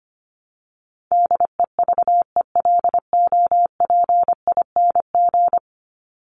Morse code, heard as a keyed tone, about 25 wpm, 700 Hz. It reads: DEVELOPING
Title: ING Suffix - Top 100 Words with Base Words 25wpm